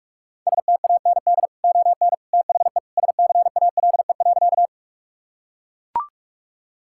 Morse code, heard as a keyed tone, about 41 wpm, 700 Hz. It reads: STAND ON THE SCALE.
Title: Sentences from Top 1000 Words - Review Part 5 - 45wpm